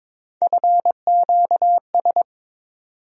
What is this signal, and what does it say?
Morse code, heard as a keyed tone, about 21 wpm, 700 Hz. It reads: FQS
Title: USA Call Sign Postfixes - 3 Letters 22wpm